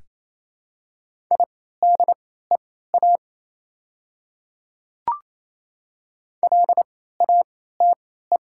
Morse code, heard as a keyed tone, about 27 wpm, spaced as for 16 wpm, 700 Hz. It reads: IDEA LATE
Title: 3x Character Spacing - Top 300 Words - 28wpm